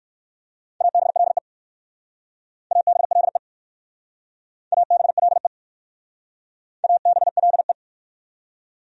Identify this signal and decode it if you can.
Morse code, heard as a keyed tone, about 54 wpm, 700 Hz. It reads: ABLE ABLE ABLE ABLE
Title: Speed-Racing Top 500 Words 45wpm